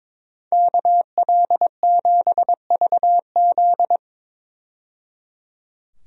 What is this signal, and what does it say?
Morse code, heard as a keyed tone, about 22 wpm, 700 Hz. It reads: KL7VZ
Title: DXCC-100 Call Signs - 5 Characters - 22wpm